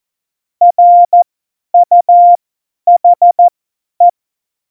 Morse code, heard as a keyed tone, about 14 wpm, spaced as for 10 wpm, 700 Hz. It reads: RUHE
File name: Morsekurs / Witze / Witz 22. Dezember 2020, 50 F70